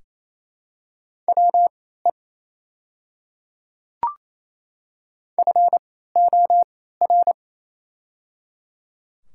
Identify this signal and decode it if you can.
Morse code, heard as a keyed tone, about 28 wpm, spaced as for 16 wpm, 700 Hz. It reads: WE FOR
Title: 3x Character Spacing - Top 200 Words - 28wpm